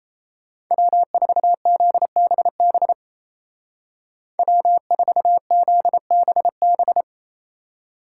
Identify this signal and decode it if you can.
Morse code, heard as a keyed tone, about 34 wpm, 700 Hz. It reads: W4ZBB W4ZBB
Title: Speed Racing - USA 1x3 Call Signs 28wpm